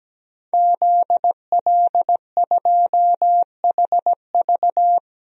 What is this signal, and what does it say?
Morse code, heard as a keyed tone, about 17 wpm, 700 Hz. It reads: ZL2HV
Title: DXCC-100 Call Signs - 5 Characters - Repeated 3x - 17wpm